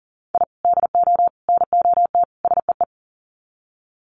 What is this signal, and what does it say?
Morse code, heard as a keyed tone, about 40 wpm, 700 Hz. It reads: I DO NOT SEE
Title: Sets of 4 Words 40wpm